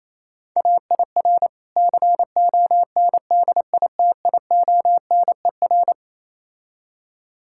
Morse code, heard as a keyed tone, about 27 wpm, 690 Hz. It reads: AIR CONDITIONER
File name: Easy Words - Speed-Racing with Hints - 22wpm